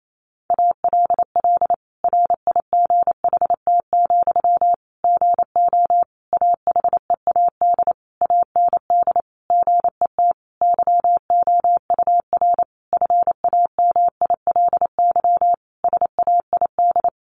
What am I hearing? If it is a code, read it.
Morse code, heard as a keyed tone, about 28 wpm, 700 Hz. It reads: ALL RIGHT, GO AHEAD AND GET YOUR FAMILY SAID
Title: Sentences from Top 1100 Words 28wpm